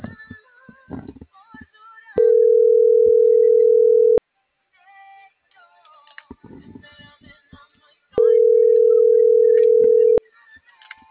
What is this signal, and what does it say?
Ringback tone (North America), 440 + 480 Hz.